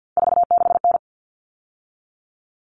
Morse code, heard as a keyed tone, about 50 wpm, 700 Hz. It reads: 46N
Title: All Cut Numbers - 3-Digit 50wpm